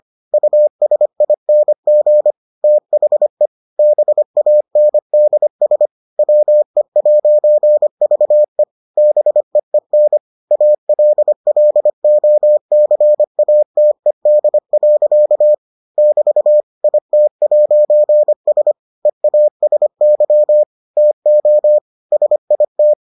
Morse code, heard as a keyed tone, about 25 wpm, 600 Hz. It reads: USING THE BANDS WE'VE BEEN ALLOCATED. <BT> IT'S EASY TO SIT